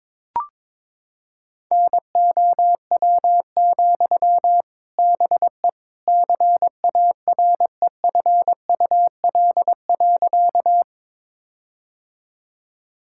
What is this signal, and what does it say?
Morse code, heard as a keyed tone, about 21 wpm, 700 Hz. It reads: NOW, BE CAREFUL.
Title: Sentences from Top 1600 Words 22wpm